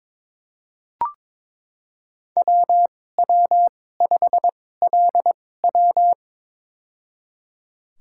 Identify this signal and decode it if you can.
Morse code, heard as a keyed tone, about 22 wpm, spaced as for 16 wpm, 700 Hz. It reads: WW5LW
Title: USA Call Signs - 5 Characters - 2x Character Spacing - 22wpm